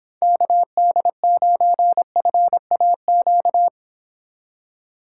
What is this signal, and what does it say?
Morse code, heard as a keyed tone, about 29 wpm, 700 Hz. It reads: KD9FAQ